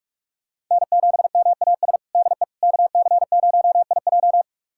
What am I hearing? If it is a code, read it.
Morse code, heard as a keyed tone, about 42 wpm, 700 Hz. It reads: N7MAS DE KC0IJ
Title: USA Callsign DE Callsign - 5 Characters - 45wpm